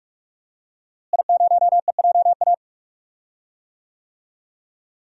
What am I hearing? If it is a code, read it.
Morse code, heard as a keyed tone, about 46 wpm, 700 Hz. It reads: I0EJA
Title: DXCC-100 Call Signs - 5 Characters - Repeated 3x - 45wpm